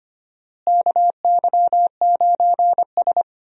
Morse code, heard as a keyed tone, about 25 wpm, 700 Hz. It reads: KY9S
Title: USA Call Signs - 4 Characters - Repeated 2x - 25wpm